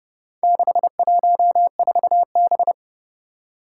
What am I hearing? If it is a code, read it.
Morse code, heard as a keyed tone, about 30 wpm, 700 Hz. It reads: 614B
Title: Four Challenging Letter-Number Combinations 30wpm